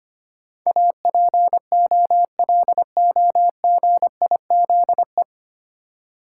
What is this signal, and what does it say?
Morse code, heard as a keyed tone, about 25 wpm, 700 Hz. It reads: APOLOGIZE
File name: practice / 25wpm / 9 Letter Words - First 3 Letters Given - 25wpm